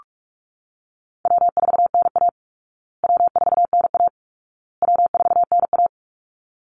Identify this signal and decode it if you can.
Morse code, heard as a keyed tone, about 44 wpm, 700 Hz. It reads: W4NA W4NA W4NA
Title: Easy Call Signs Repeated 3x - 45wpm